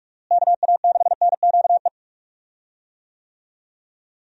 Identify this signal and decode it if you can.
Morse code, heard as a keyed tone, about 46 wpm, 700 Hz. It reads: KA6NQE
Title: USA Call Signs - 6 Characters - Repeated 3x - 45wpm